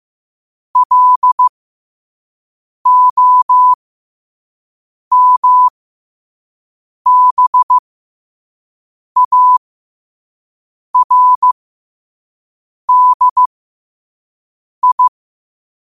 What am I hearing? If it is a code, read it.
Morse code, heard as a keyed tone, about 15 wpm, spaced as for 5 wpm, 1000 Hz. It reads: LOMBARDI
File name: qotdfiles / 5wpm